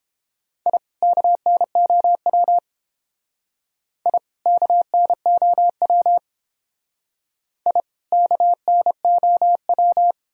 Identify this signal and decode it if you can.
Morse code, heard as a keyed tone, about 30 wpm, 700 Hz. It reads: I KNOW I KNOW I KNOW